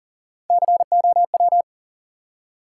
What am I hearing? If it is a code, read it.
Morse code, heard as a keyed tone, about 42 wpm, 700 Hz. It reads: COW
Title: Three Letter Words - 40wpm